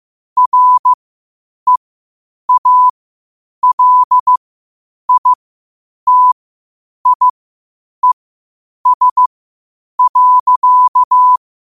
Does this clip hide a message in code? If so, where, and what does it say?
Yes: Morse code from 0.4 to 11.4 s, heard as a keyed tone, about 15 wpm, spaced as for 8 wpm, 1000 Hz. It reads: REALITIES.